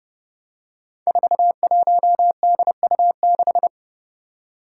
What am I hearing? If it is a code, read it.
Morse code, heard as a keyed tone, about 30 wpm, 700 Hz. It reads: 41DU6
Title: Five Challenging Letter-Number Combinations 30wpm